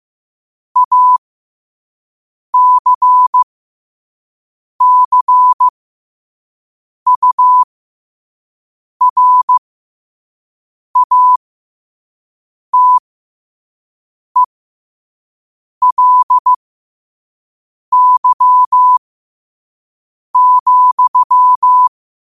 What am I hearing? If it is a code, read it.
Morse code, heard as a keyed tone, about 15 wpm, spaced as for 5 wpm, 1000 Hz. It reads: ACCURATELY,
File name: qotdfiles / 5wpm